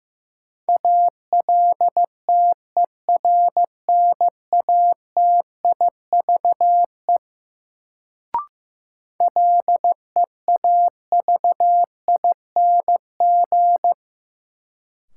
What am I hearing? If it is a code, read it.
Morse code, heard as a keyed tone, about 15 wpm, 700 Hz. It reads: ALTERNATIVE LEAVING
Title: Sentences from Top 1000 Words - Review Part 4 - New Word Drill - 15wpm